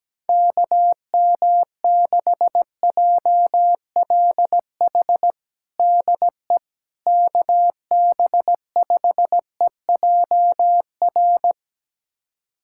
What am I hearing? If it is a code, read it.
Morse code, heard as a keyed tone, about 17 wpm, 700 Hz. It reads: KM6JLH DE KB5EJR
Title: USA Callsign DE Callsign - 6 Characters - 17wpm